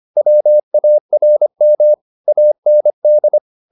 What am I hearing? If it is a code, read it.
Morse code, heard as a keyed tone, about 25 wpm, 600 Hz. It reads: WARM AND
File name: the old man and the sea part 1 25wpm